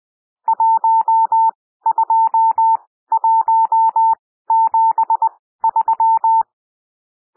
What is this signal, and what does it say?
Morse code, heard as a keyed tone, about 20 wpm, spaced as for 14 wpm, 900 Hz. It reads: 12173
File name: cwmp3 / N1 SOUND NUMBERS 20 wpm